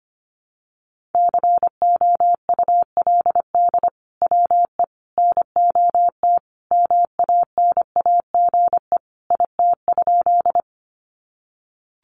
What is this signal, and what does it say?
Morse code, heard as a keyed tone, about 25 wpm, 700 Hz. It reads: COULD WE NOT MANAGE IT?
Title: Sentences from Top 1400 Words 25wpm